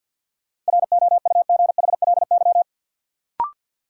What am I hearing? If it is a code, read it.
Morse code, heard as a keyed tone, about 53 wpm, 700 Hz. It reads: ROUGHLY
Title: Sentences from Top 1500 Words - New Word Drill - 50wpm